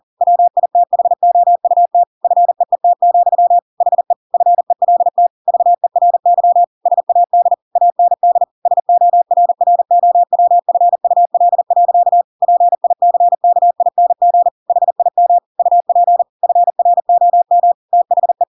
Morse code, heard as a keyed tone, about 40 wpm, 700 Hz. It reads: WITHOUT FEET, HE FELT VERY SAD AND SORROWFUL. PICKING HIM UP FROM THE
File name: Pinocchio / Pinocchio ch7 40 40 ik1hge